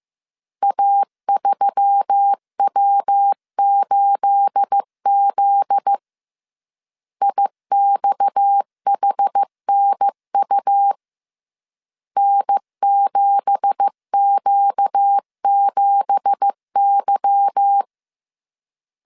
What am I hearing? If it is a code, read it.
Morse code, heard as a keyed tone, about 15 wpm, 800 Hz. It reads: A3W8Z IXHNU N7Q7Y
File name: CW / DET / 1 9 1500C S A 14wpm 3222